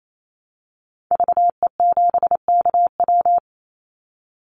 Morse code, heard as a keyed tone, about 28 wpm, 700 Hz. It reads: VE7KW
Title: Call Signs - 28wpm